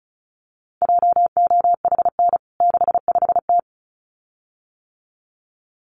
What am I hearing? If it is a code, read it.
Morse code, heard as a keyed tone, about 34 wpm, 700 Hz. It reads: JOHN 65T
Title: CWT Exchange 35wpm